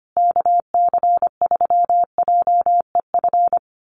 Morse code, heard as a keyed tone, about 25 wpm, 700 Hz. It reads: KC3JEF
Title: USA Call Signs Repeated 3x 25wpm